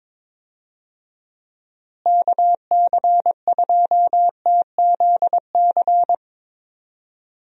Morse code, heard as a keyed tone, about 22 wpm, 700 Hz. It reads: KC2TZC
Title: USA Call Signs - 6 Characters - Repeated 2x - 22wpm